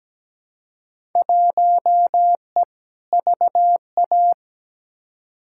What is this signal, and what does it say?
Morse code, heard as a keyed tone, about 17 wpm, 700 Hz. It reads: 1E VA
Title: Field Day Exchanges - 17wpm